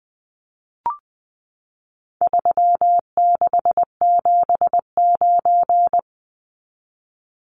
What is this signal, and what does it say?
Morse code, heard as a keyed tone, about 20 wpm, 700 Hz. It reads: 3679
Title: Numbers - 4-Digit 20wpm